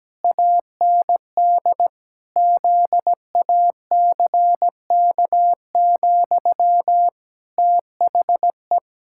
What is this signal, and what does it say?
Morse code, heard as a keyed tone, about 17 wpm, 700 Hz. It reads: AND ZACK, THE
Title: Pangram Challenge v2 - 17wpm